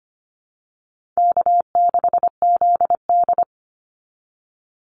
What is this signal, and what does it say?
Morse code, heard as a keyed tone, about 25 wpm, 700 Hz. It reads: K6ZD
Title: Easy Call Signs Repeated 2x - 25wpm